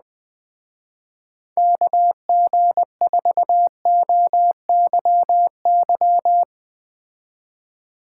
Morse code, heard as a keyed tone, about 20 wpm, 700 Hz. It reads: KG4OYY
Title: Speed Racing - USA 2x3 Call Signs 17wpm